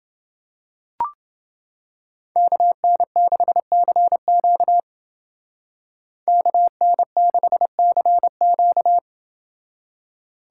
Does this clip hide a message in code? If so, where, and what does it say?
Yes: Morse code from 2.4 to 9.0 s, heard as a keyed tone, about 29 wpm, 700 Hz. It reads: KN6CQ KN6CQ